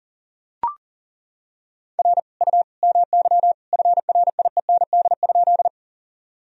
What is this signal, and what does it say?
Morse code, heard as a keyed tone, about 40 wpm, 700 Hz. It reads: R U MY FRIEND?
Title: Sentences from Top 300 Words and Abbreviations 40wpm